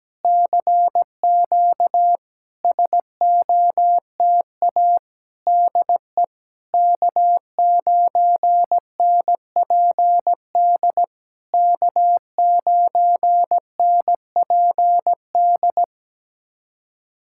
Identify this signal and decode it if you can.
Morse code, heard as a keyed tone, about 17 wpm, 700 Hz. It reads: CQ SOTA DE K9NPD K9NPD
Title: SOTA QSO Practice 17wpm